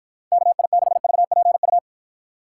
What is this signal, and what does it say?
Morse code, heard as a keyed tone, about 53 wpm, 700 Hz. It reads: KI6VPU